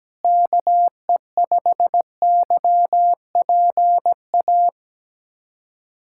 Morse code, heard as a keyed tone, about 17 wpm, 700 Hz. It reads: KE5YPA
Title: USA Callsign DE Callsign 17wpm